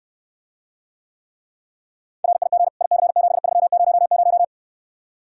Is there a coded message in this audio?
Yes: Morse code from 2.2 to 4.4 s, heard as a keyed tone, about 69 wpm, 700 Hz. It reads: RIG IC7300